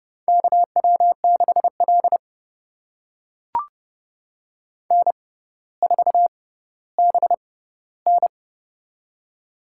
Morse code, heard as a keyed tone, about 30 wpm, spaced as for 15 wpm, 690 Hz. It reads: KW6L N4BN
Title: USA Call Signs - 4 Characters - 6x Character Spacing - 30wpm